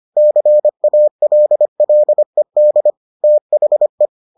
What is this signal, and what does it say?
Morse code, heard as a keyed tone, about 25 wpm, 600 Hz. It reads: CALLED THE